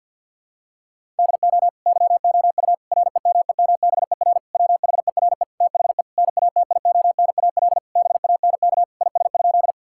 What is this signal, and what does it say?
Morse code, heard as a keyed tone, about 50 wpm, 700 Hz. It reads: DO YOU REMEMBER WHERE THE NATIONAL BANK IS?
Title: Sentences from Top 1000 Words - Review Part 6 - 50wpm